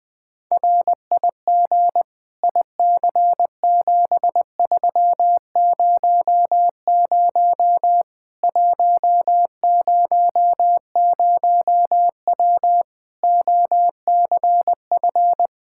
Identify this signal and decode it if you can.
Morse code, heard as a keyed tone, about 20 wpm, 700 Hz. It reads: RIG IC7300 100W OCF
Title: Practice QSO - Exchange 2 - Info - 20wpm